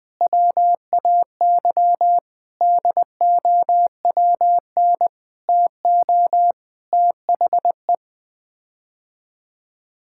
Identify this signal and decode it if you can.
Morse code, heard as a keyed tone, about 21 wpm, 700 Hz. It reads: WAY DOWN TO THE
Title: Sets of 6 Words - 20wpm